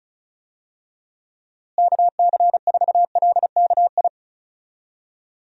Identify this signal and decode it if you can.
Morse code, heard as a keyed tone, about 34 wpm, 700 Hz. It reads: KC4LKI
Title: USA Call Signs - 6 Characters - Repeated 3x - 35wpm